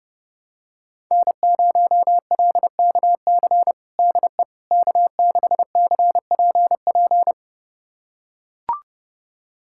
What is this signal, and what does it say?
Morse code, heard as a keyed tone, about 32 wpm, 700 Hz. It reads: N0LKC DE K6CPP
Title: USA Callsign DE Callsign - 5 Characters - 30wpm